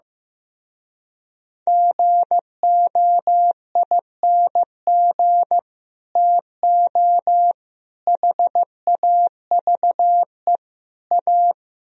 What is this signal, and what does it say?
Morse code, heard as a keyed tone, about 15 wpm, 700 Hz. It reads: GOING TO HAVE A
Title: Speed-Racing - Sets of 4 Words 15wpm